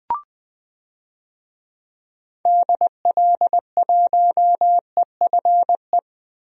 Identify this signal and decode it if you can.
Morse code, heard as a keyed tone, about 20 wpm, 700 Hz. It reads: DL1EFE
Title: DXCC-100 Call Signs - 6 Characters - Repeated 2x - 20wpm